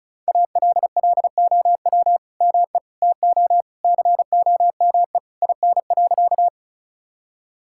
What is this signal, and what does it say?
Morse code, heard as a keyed tone, about 36 wpm, 700 Hz. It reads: ALLOW ME TO COME IN.
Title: Sentences from Top 1000 Words - Review Part 5 - 35wpm